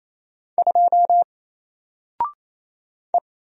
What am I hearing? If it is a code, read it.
Morse code, heard as a keyed tone, about 27 wpm, 700 Hz. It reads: 2 E
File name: Single-Letter-Number-28wpm